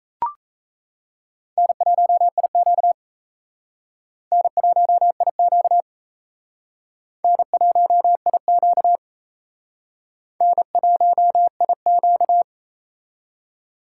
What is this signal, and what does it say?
Morse code, heard as a keyed tone, about 35 wpm, 700 Hz. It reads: N1IQ N1IQ N1IQ N1IQ